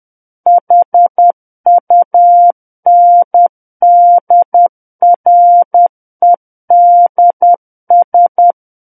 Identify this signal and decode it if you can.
Morse code, heard as a keyed tone, about 10 wpm, 700 Hz. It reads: HUNDREDS
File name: Book One / 01-Chapter 1 - 10 WPM